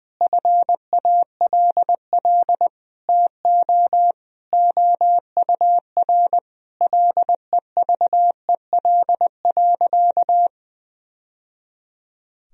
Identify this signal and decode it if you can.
Morse code, heard as a keyed tone, about 20 wpm, 700 Hz. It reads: FALL TO OUR LEVEL.
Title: Sentences from Top 1000 Words - Review Part 4 - 20wpm